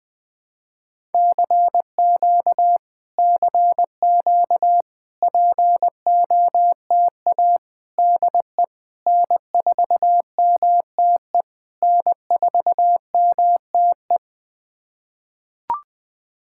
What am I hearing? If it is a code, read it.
Morse code, heard as a keyed tone, about 20 wpm, 660 Hz. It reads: CQ CQ POTA DE N4MTE N4MTE
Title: POTA QSO Practice - 20wpm